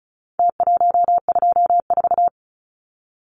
Morse code, heard as a keyed tone, about 34 wpm, 700 Hz. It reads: T124
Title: Numbers - Contest Style - 4-Digit 35wpm